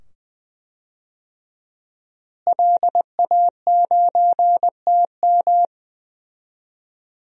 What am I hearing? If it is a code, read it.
Morse code, heard as a keyed tone, about 20 wpm, 700 Hz. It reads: LA9TM
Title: DXCC-100 Call Signs - 5 Characters - Repeated 2x - 20wpm